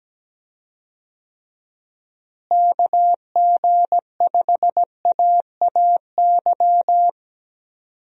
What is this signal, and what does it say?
Morse code, heard as a keyed tone, about 17 wpm, 700 Hz. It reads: KG5AAY
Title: USA Call Signs - 6 Characters - Repeated 3x - 17wpm